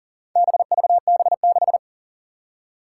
Morse code, heard as a keyed tone, about 40 wpm, 700 Hz. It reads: BVB6
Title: Four Challenging Letter-Number Combinations 40wpm